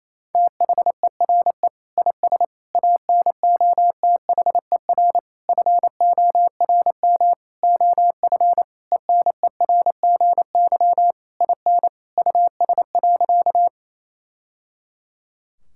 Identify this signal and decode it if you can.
Morse code, heard as a keyed tone, about 27 wpm, 700 Hz. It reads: THERE IS ANOTHER FORM OF ENERGY IN US.
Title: Sentences from Top 1000 Words - Review Part 6 - 28wpm